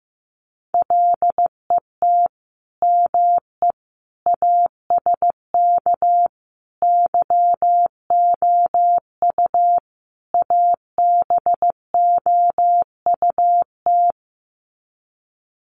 Morse code, heard as a keyed tone, about 15 wpm, 700 Hz. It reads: LET ME ASK YOU ABOUT
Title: Sets of 5 Words 15wpm